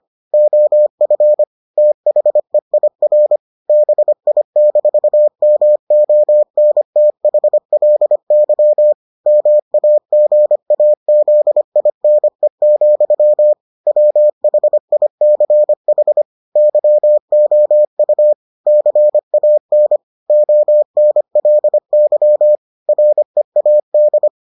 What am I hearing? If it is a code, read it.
Morse code, heard as a keyed tone, about 25 wpm, 600 Hz. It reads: OF THEIR BI-MONTHLY MAGAZINE, WHICH YOU CAN ONLY READ